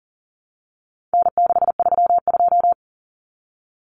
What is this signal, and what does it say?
Morse code, heard as a keyed tone, about 40 wpm, 700 Hz. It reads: N632
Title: Common Cut Numbers - 4-Digit 40wpm